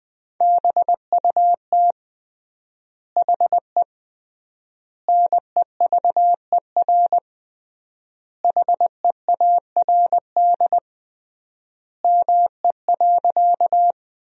Morse code, heard as a keyed tone, about 20 wpm, 700 Hz. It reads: BUT HE NEVER HEARD ME.
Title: 3x Word-Spacing - Sentences from Top 400 Words - 20wpm